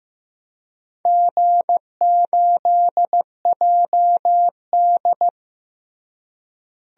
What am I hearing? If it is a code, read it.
Morse code, heard as a keyed tone, about 15 wpm, 700 Hz. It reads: G8JD
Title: DXCC-100 Call Signs - 4 Characters - Repeated 2x - 15wpm